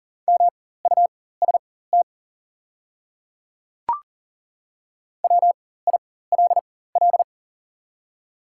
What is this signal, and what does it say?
Morse code, heard as a keyed tone, about 40 wpm, spaced as for 19 wpm, 700 Hz. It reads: MUST WILL
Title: 4x Character Spacing - Top 200 Words - 40wpm